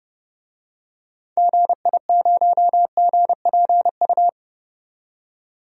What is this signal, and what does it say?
Morse code, heard as a keyed tone, about 30 wpm, 700 Hz. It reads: GI0GPU
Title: DXCC-100 Call Signs - 6 Characters - 30wpm